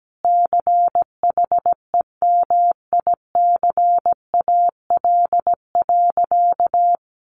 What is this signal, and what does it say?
Morse code, heard as a keyed tone, about 17 wpm, 700 Hz. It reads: CHEMICAL.
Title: Sentences from Top 1200 Words 17wpm